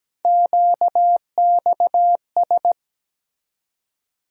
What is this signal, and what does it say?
Morse code, heard as a keyed tone, about 17 wpm, 700 Hz. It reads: QXS
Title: USA Call Sign Postfixes - 3 Letters 17wpm